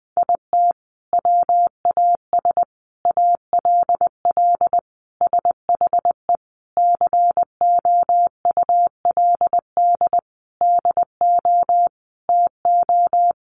Morse code, heard as a keyed tone, about 20 wpm, 700 Hz. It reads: IT WAS ALL SHE COULD DO TO
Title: Sentences from Top 400 Words 20wpm